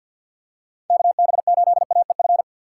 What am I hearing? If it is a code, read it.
Morse code, heard as a keyed tone, about 50 wpm, 700 Hz. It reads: KB8AEF
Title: USA Call Signs - 6x Character Spacing - 50wpm